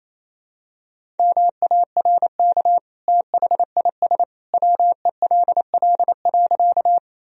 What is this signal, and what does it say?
Morse code, heard as a keyed tone, about 28 wpm, 700 Hz. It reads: MARK THIS WELL.
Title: Sentences from Top 1000 Words - Review Part 1 - 28wpm